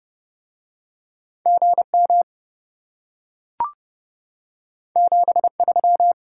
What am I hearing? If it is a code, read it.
Morse code, heard as a keyed tone, about 30 wpm, 700 Hz. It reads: GM 73